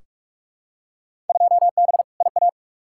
Morse code, heard as a keyed tone, about 46 wpm, 700 Hz. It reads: 2B IA
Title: Field Day Exchanges Encore - 45wpm